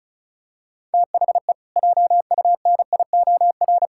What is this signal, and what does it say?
Morse code, heard as a keyed tone, about 36 wpm, 700 Hz. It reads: THE JUNIOR
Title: Pangram Challenge v2 - 35wpm